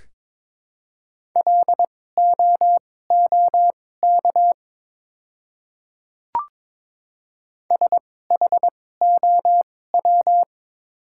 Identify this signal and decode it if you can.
Morse code, heard as a keyed tone, about 22 wpm, spaced as for 16 wpm, 700 Hz. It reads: LOOK SHOW
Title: 2x Character Spacing - Top 500 Words - 22wpm